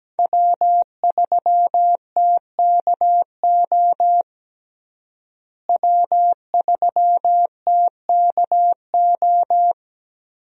Morse code, heard as a keyed tone, about 17 wpm, 700 Hz. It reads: W3TKO W3TKO
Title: USA Call Signs - 5 Characters - Repeated 3x - 17wpm